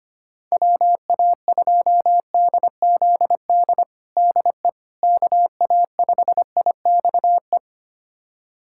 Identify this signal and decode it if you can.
Morse code, heard as a keyed tone, about 25 wpm, 700 Hz. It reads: WA2DZD DE KA5IXE
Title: USA Callsign DE Callsign - 6 Characters - 25wpm